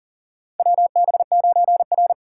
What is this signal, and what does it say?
Morse code, heard as a keyed tone, about 40 wpm, 700 Hz. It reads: WB9R
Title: USA Call Signs - 4 Characters - 2x Character Spacing - 40wpm